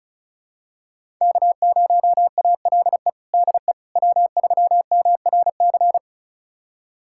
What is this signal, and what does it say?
Morse code, heard as a keyed tone, about 34 wpm, 700 Hz. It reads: K0ALE DE W3MRC
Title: USA Callsign DE Callsign - 5 Characters - 35wpm